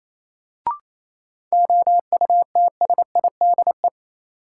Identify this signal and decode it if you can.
Morse code, heard as a keyed tone, about 28 wpm, 700 Hz. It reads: OUTSIDE
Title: Sentences from Top 900 Words - New Word Drill - 28wpm